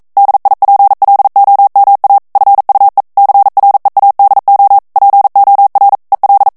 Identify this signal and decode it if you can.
Morse code, heard as a keyed tone, about 42 wpm, 800 Hz. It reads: DIPLOMA FUE CREADO POR EL
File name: morse mp3 / ESPAGNOL TEXTE EN CLAIR 1 VITESSE 40 MOTS MINUTES